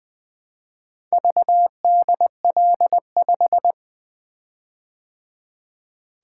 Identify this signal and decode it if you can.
Morse code, heard as a keyed tone, about 20 wpm, 700 Hz. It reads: VDL5
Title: Four Letter-Number Combinations - 20wpm